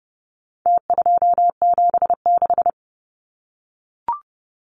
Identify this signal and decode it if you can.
Morse code, heard as a keyed tone, about 30 wpm, 700 Hz. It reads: T276
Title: Numbers - Contest Style - 4-Digit 30wpm